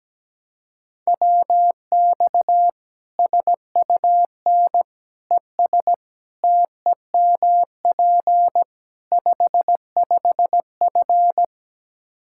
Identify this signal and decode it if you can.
Morse code, heard as a keyed tone, about 17 wpm, 700 Hz. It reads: WX SUN ES TEMP 55F
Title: Practice QSO - Exchange 2 - Info - 17wpm